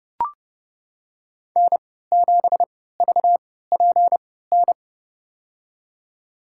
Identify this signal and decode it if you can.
Morse code, heard as a keyed tone, about 30 wpm, spaced as for 17 wpm, 700 Hz. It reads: N7VPN
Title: USA Call Signs - 5 Characters - 3x Character Spacing - 30wpm